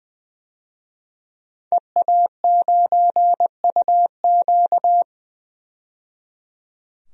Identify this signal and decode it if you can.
Morse code, heard as a keyed tone, about 21 wpm, 700 Hz. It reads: EA9UQ
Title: DXCC-100 Call Signs - 5 Characters - 20wpm